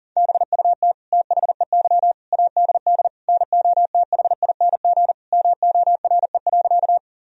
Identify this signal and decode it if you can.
Morse code, heard as a keyed tone, about 40 wpm, 700 Hz. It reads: BUT THEY ADD NOTHING MORE.
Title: Sentences from Top 1000 Words - Review Part 1 - 40wpm